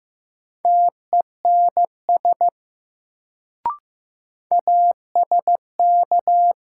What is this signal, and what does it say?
Morse code, heard as a keyed tone, about 15 wpm, 700 Hz. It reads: TENS ASK
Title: Sets of 4 Words - New Word Drill - 15wpm